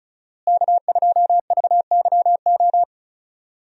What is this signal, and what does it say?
Morse code, heard as a keyed tone, about 36 wpm, 700 Hz. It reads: K2VYO
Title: USA Call Signs - 5 Characters - Repeated 3x - 35wpm